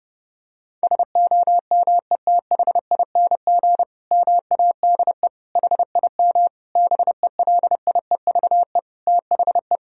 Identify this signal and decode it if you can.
Morse code, heard as a keyed tone, about 30 wpm, 700 Hz. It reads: SOMETHING MADE HIM BELIEVE THE
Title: Sentences from Top 1000 Words - Review Part 1 - 30wpm